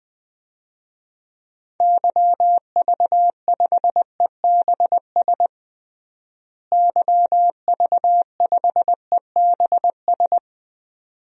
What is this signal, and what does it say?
Morse code, heard as a keyed tone, about 20 wpm, 700 Hz. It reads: YV5EBS YV5EBS
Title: DXCC-100 Call Signs - 6 Characters - Repeated 2x - 20wpm